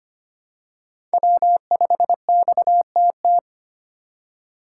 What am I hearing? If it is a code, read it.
Morse code, heard as a keyed tone, about 25 wpm, 700 Hz. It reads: W5XTT
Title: USA Call Signs - 5 Characters - 2x Character Spacing - 25wpm